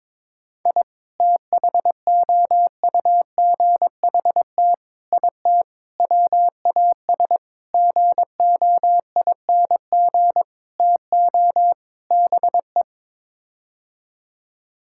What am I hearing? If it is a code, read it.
Morse code, heard as a keyed tone, about 22 wpm, 700 Hz. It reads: I THOUGHT IT WAS GOING TO BE
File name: practice / 22wpm / Sets of 7 Words - 22wpm